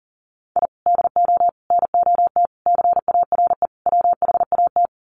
Morse code, heard as a keyed tone, about 40 wpm, 700 Hz. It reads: I DO NOT CARE WHAT
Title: Sets of 5 Words 40wpm